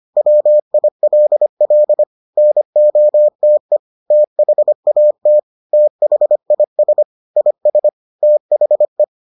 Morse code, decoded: WILL NOTE THAT THIS IS THE

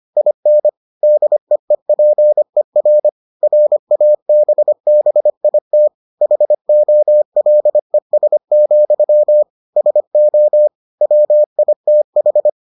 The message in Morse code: IN DEEPER RABBIT HOLES, SO WITH

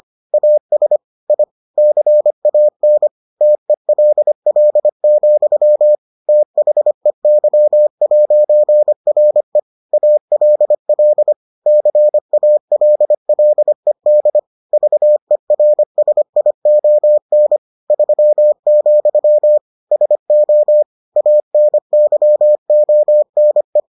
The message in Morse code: AS I CAN TELL, THEY'RE ALL CALLED VERSION 3, SO ANYONE

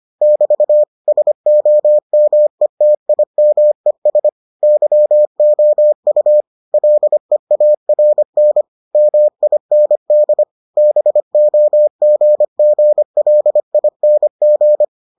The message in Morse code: <BT> SOMETIMES YOU LEARN MIND BOGGLING

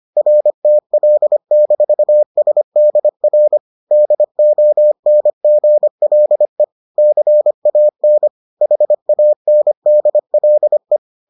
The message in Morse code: RTL-SDR DONGLE CAN HANDLE

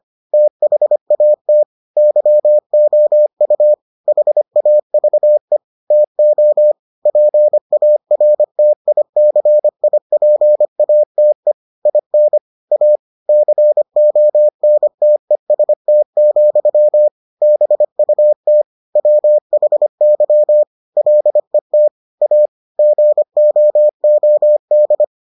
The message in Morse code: THAT YOU HAVE TO PARTICIPATE IN A CONTEST, BUT WHY LET A GOOD

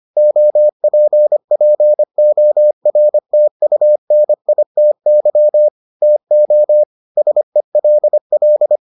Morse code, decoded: OPPORTUNITY TO SELL

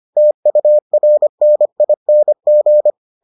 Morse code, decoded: TURNING

